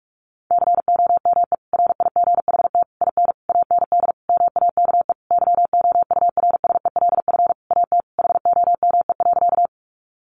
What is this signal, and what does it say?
Morse code, heard as a keyed tone, about 44 wpm, 700 Hz. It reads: COME RIGHT IN AND MAKE YOURSELF AT HOME.
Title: Sentences from Top 600 Words 45wpm